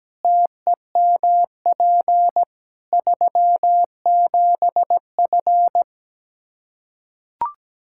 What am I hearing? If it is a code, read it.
Morse code, heard as a keyed tone, about 17 wpm, 700 Hz. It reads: TEMP 37F
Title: K1USN SST QSO Practice 17wpm